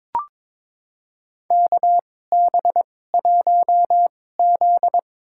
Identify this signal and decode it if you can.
Morse code, heard as a keyed tone, about 23 wpm, spaced as for 16 wpm, 700 Hz. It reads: KB1Z
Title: USA Call Signs - 4 Characters - 2x Character Spacing - 22wpm